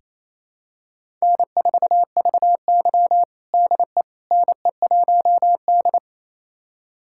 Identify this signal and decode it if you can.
Morse code, heard as a keyed tone, about 27 wpm, 690 Hz. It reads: N4VY DE NE1D
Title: USA Callsign DE Callsign - 4 Characters - 28wpm